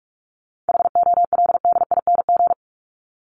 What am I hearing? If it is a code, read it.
Morse code, heard as a keyed tone, about 44 wpm, 700 Hz. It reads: HOLDING
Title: ING Suffix - Top 100 Words with Base Words 45wpm